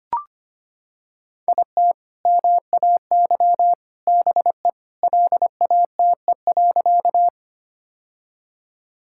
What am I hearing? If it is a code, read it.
Morse code, heard as a keyed tone, about 25 wpm, 700 Hz. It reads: IT MAY BE LATE.
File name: practice / 25wpm / Sentences with 3-Word Phrases - 25wpm